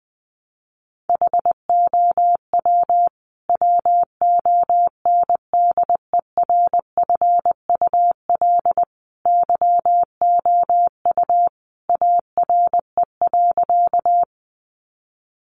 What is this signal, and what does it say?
Morse code, heard as a keyed tone, about 19 wpm, 700 Hz. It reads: HOW WONDERFUL YOU ARE.
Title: Sentences from Top 1100 Words 20wpm